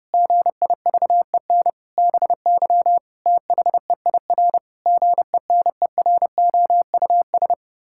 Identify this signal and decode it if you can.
Morse code, heard as a keyed tone, about 30 wpm, 700 Hz. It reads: GIVEN BY THEIR GENEROUS
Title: Pangram Challenge v2 - 30wpm